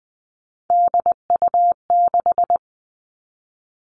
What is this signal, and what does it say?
Morse code, heard as a keyed tone, about 20 wpm, 700 Hz. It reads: DU6
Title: Three Challenging Letter-Number Combinations 20wpm